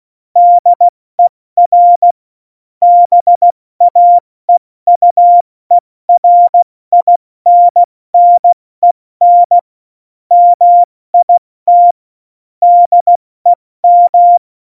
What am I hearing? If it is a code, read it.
Morse code, heard as a keyed tone, about 16 wpm, spaced as for 14 wpm, 700 Hz. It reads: DER BAEUERINNEN MIT DEM
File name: Morsekurs / Witze / Witz 21 Juli 2020, 70 F80